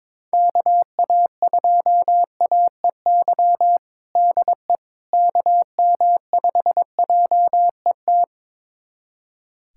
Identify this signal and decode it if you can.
Morse code, heard as a keyed tone, about 22 wpm, 700 Hz. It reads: KA2AEY DE KM5JET
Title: USA Callsign DE Callsign - 6 Characters - 22wpm